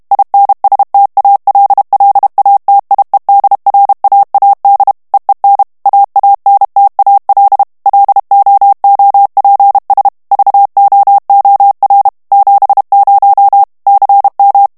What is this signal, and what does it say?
Morse code, heard as a keyed tone, about 32 wpm, 800 Hz. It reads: INSTALLATIEDRAAD EEN AANTAL LOOPS VOOR 70 CM
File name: HOLLANDAIS texte clair 1 vitesse 30 mots minute